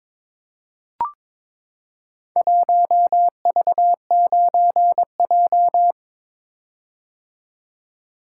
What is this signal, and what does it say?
Morse code, heard as a keyed tone, about 23 wpm, 700 Hz. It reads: 1V9J
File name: practice / 22wpm / Four Letter-Number Combinations - 22wpm